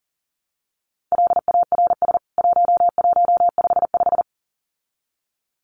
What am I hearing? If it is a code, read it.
Morse code, heard as a keyed tone, about 39 wpm, 700 Hz. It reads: LARS 1155
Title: CWT Exchange 40wpm